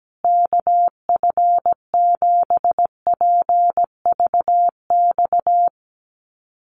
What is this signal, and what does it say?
Morse code, heard as a keyed tone, about 17 wpm, 700 Hz. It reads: KF7PVX
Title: USA Call Signs Repeated 3x 17wpm